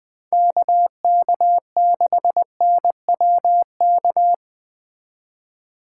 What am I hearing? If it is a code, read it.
Morse code, heard as a keyed tone, about 21 wpm, 700 Hz. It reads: KK6NWK